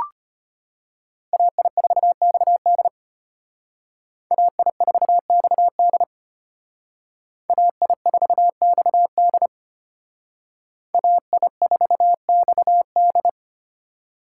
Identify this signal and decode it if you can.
Morse code, heard as a keyed tone, about 31 wpm, 700 Hz. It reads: AI4XD AI4XD AI4XD AI4XD
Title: Speed Racing - USA Call Signs 25wpm